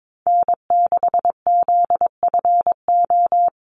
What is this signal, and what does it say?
Morse code, heard as a keyed tone, about 22 wpm, 700 Hz. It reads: N6ZFO
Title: Call Signs Repeated 3x - 22wpm